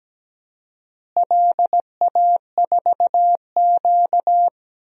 Morse code, heard as a keyed tone, about 17 wpm, 700 Hz. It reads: LA4Q
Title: DXCC-100 Call Signs - 4 Characters - Repeated 2x - 17wpm